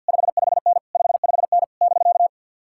Morse code, heard as a keyed tone, about 66 wpm, spaced as for 58 wpm, 700 Hz. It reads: 55N 55N <BK>
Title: Speed-Racing Common QSO Phrases Encore - 50wpm